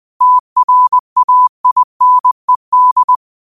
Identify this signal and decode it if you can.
Morse code, heard as a keyed tone, about 20 wpm, 1000 Hz. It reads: TRAINED
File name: qotdfiles / 20wpm